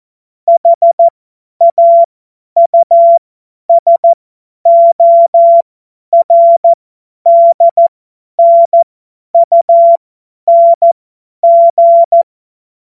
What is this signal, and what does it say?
Morse code, heard as a keyed tone, about 14 wpm, spaced as for 10 wpm, 680 Hz. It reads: HAUSORDNUNG
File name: Morsekurs / Witze / Witz 22. Juni 2021, 50 F70